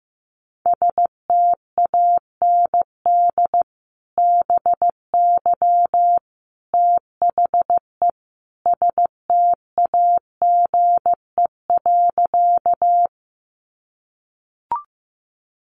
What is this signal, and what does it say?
Morse code, heard as a keyed tone, about 14 wpm, 700 Hz. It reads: STAND BY THE STAGE.
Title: Sentences from Top 1200 Words 15wpm